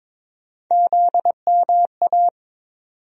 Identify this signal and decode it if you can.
Morse code, heard as a keyed tone, about 23 wpm, 700 Hz. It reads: ZMA
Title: USA Call Sign Postfixes - 3 Letters 22wpm